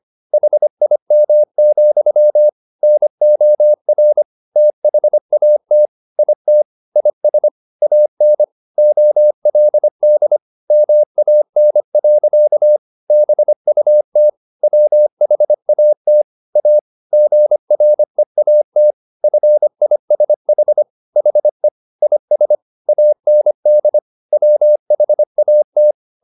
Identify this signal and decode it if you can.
Morse code, heard as a keyed tone, about 25 wpm, 600 Hz. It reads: HIM, NOR THAT IT IS AN OLD MAN. BUT WHAT A GREAT FISH HE IS AND WHAT
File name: the old man and the sea part 1 25wpm